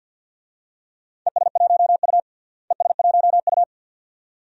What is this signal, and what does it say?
Morse code, heard as a keyed tone, about 47 wpm, 700 Hz. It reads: ES1U ES1U
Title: DXCC-100 Call Signs - 4 Characters - Repeated 2x - 50wpm